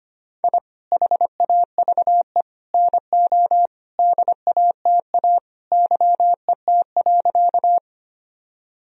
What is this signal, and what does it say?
Morse code, heard as a keyed tone, about 25 wpm, 700 Hz. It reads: I HAVE NO DATA YET.
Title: Sentences from Top 1000 Words - Review Part 3 - 25wpm